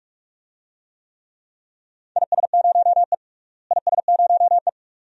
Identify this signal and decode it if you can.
Morse code, heard as a keyed tone, about 42 wpm, 700 Hz. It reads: IS0E IS0E
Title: DXCC-100 Call Signs - 4 Characters - Repeated 2x - 45wpm